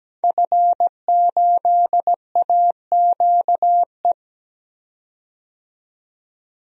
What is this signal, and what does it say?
Morse code, heard as a keyed tone, about 17 wpm, 700 Hz. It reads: F8AQE
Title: DXCC-100 Call Signs - 5 Characters - 17wpm